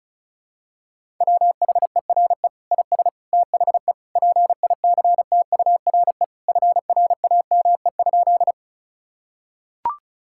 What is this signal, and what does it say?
Morse code, heard as a keyed tone, about 34 wpm, 700 Hz. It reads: WHERE IS THE PICTURE FRAME?
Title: Sentences from Top 1000 Words - Review Part 6 - 35wpm